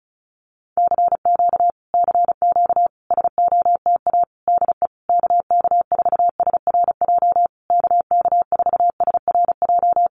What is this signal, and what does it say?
Morse code, heard as a keyed tone, about 34 wpm, 700 Hz. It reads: CQ CQ SOTA DE KK4SRJ KK4SRJ
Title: Common QSO Phrases Encore 35wpm